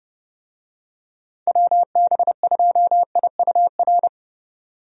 Morse code, heard as a keyed tone, about 29 wpm, 700 Hz. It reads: WB2IUR
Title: USA Call Signs - 6 Characters - 30wpm